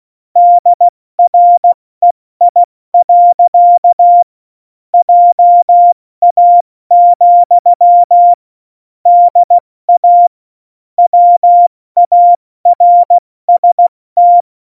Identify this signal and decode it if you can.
Morse code, heard as a keyed tone, about 16 wpm, 700 Hz. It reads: DREI. JA, DA WARST
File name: Morsekurs / Witze / Witz 12. September 17, Tempo 70